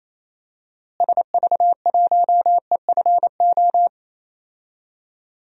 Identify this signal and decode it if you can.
Morse code, heard as a keyed tone, about 28 wpm, 700 Hz. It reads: SV1EFO
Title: DXCC-100 Call Signs - 6 Characters - Repeated 3x - 28wpm